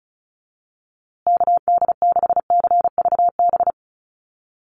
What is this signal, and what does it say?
Morse code, heard as a keyed tone, about 34 wpm, 700 Hz. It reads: KD6CVB
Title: USA Call Signs Repeated 3x 35wpm